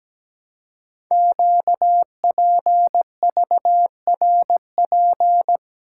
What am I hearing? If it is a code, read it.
Morse code, heard as a keyed tone, about 17 wpm, 700 Hz. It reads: QPVRP
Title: Code Groups - 5 Letters - 17wpm